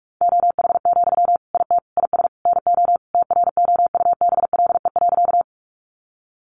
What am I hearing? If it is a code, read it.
Morse code, heard as a keyed tone, about 44 wpm, 700 Hz. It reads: OH, IT IS NO TROUBLE.
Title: Sentences from Top 1200 Words 45wpm